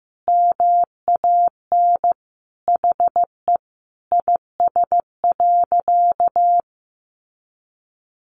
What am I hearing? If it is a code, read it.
Morse code, heard as a keyed tone, about 15 wpm, 700 Hz. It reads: MAN HE IS.
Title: Sentences from Top 1100 Words 15wpm